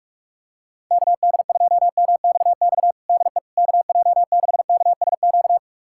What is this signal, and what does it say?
Morse code, heard as a keyed tone, about 46 wpm, 700 Hz. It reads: KD2MXX DE KJ6KSQ
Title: USA Callsign DE Callsign 45wpm